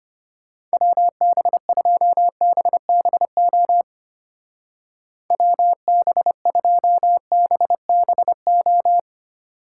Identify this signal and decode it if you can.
Morse code, heard as a keyed tone, about 31 wpm, 700 Hz. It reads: WB2BBO WB2BBO